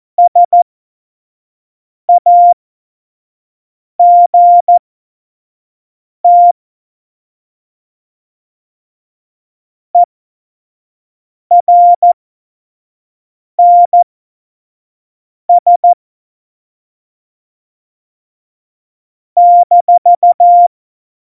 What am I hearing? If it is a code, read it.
Morse code, heard as a keyed tone, about 14 wpm, spaced as for 5 wpm, 700 Hz. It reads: SAGT ERNS -
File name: Morsekurs / Witze / Witz 19. Dezember 2017, 25 F70